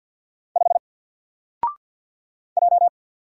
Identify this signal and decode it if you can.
Morse code, heard as a keyed tone, about 48 wpm, 700 Hz. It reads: 5 J
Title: Single Character (Letter:Number:Punctuation:Prosign) Alt 50wpm